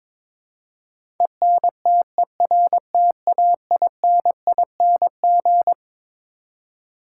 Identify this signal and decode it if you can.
Morse code, heard as a keyed tone, about 22 wpm, 700 Hz. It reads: ENTERTAINING
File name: Twelve Letter Words - 22wpm